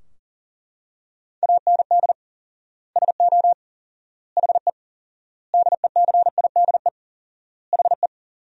Morse code, heard as a keyed tone, about 40 wpm, 700 Hz. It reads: AND SO HE DECIDE HE
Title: 4x Word-Spacing - Sentences from Top 500 Words - 40wpm